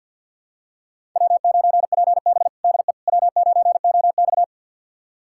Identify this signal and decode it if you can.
Morse code, heard as a keyed tone, about 48 wpm, 700 Hz. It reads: W9PB DE W9OX
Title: USA Callsign DE Callsign - 4 Characters - 50wpm